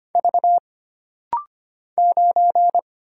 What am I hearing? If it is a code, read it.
Morse code, heard as a keyed tone, about 25 wpm, 700 Hz. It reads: V 9
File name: Single Character (Letter:Number:Punctuation:Prosign) - Rapid-Fire - 25wpm